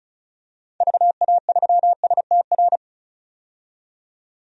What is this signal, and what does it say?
Morse code, heard as a keyed tone, about 34 wpm, 700 Hz. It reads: VA3STR
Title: DXCC-100 Call Signs - 6 Characters - 35wpm